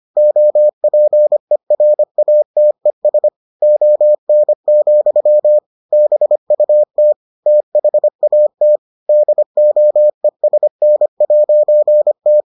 Morse code, heard as a keyed tone, about 25 wpm, 600 Hz. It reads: OPERATES ON, BUT THAT DOESN'T